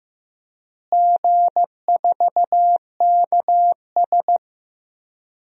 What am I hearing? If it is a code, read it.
Morse code, heard as a keyed tone, about 16 wpm, 700 Hz. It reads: G4KS
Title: DXCC-100 Call Signs - 4 Characters - Repeated 2x - 15wpm